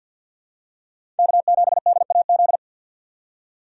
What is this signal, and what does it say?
Morse code, heard as a keyed tone, about 50 wpm, 700 Hz. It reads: K7DAZ
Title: USA Call Signs - 5 Characters - 50wpm